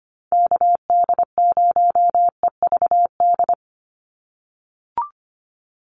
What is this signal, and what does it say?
Morse code, heard as a keyed tone, about 25 wpm, 700 Hz. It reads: KD0EVD
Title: USA Call Signs 25wpm